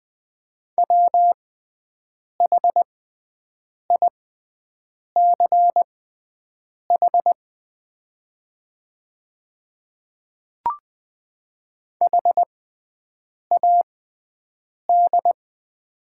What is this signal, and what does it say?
Morse code, heard as a keyed tone, about 20 wpm, spaced as for 7 wpm, 700 Hz. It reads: WHICH HAD
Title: 6x Character Spacing - Top 100 Words - 20wpm